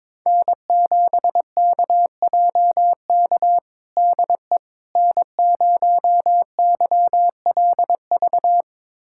Morse code, heard as a keyed tone, about 22 wpm, 700 Hz. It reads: N7KJK DE N0YLV
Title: USA Callsign DE Callsign - 5 Characters - 22wpm